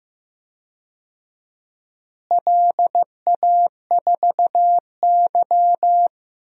Morse code, heard as a keyed tone, about 15 wpm, 700 Hz. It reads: LA4Y
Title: DXCC-100 Call Signs - 4 Characters - Repeated 3x - 15wpm